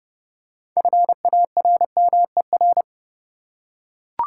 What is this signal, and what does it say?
Morse code, heard as a keyed tone, about 30 wpm, 700 Hz. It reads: FARMER
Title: Six Letter Words - 30wpm